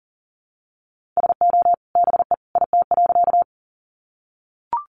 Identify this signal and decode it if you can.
Morse code, heard as a keyed tone, about 39 wpm, 700 Hz. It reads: SO BE IT.
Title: Sentences from Top 100 Words 40wpm